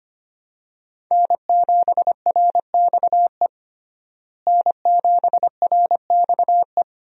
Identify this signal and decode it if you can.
Morse code, heard as a keyed tone, about 25 wpm, 700 Hz. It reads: N7RXE N7RXE
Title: USA Call Signs - 5 Characters - Repeated 2x - 25wpm